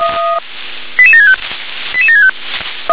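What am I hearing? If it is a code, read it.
Five-tone selective call (ZVEI), heard as quick runs of five tones, tones of about 70 ms. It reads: 88765 88765
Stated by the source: ZVEI five-tone selective call received on an SDR.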